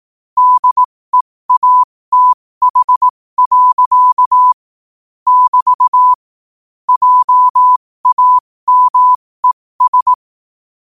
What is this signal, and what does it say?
Morse code, heard as a keyed tone, about 18 wpm, spaced as for 15 wpm, 1000 Hz. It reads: DEATH. <BT> JAMES
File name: qotdfiles / 15wpm